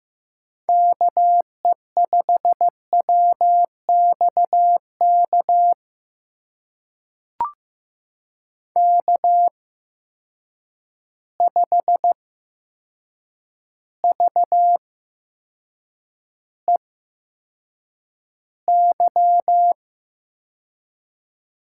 Morse code, heard as a keyed tone, about 15 wpm, spaced as for 7 wpm, 700 Hz. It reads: KE5WXK K5VEY
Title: USA Call Signs - 8x Character Spacing - 15wpm